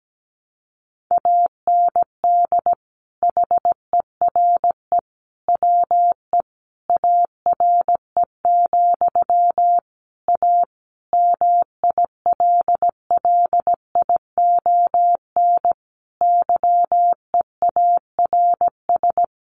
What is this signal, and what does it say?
Morse code, heard as a keyed tone, about 17 wpm, 700 Hz. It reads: AND HERE WE ARE, A MILLION YEARS
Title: A Sound Of Thunder 17wpm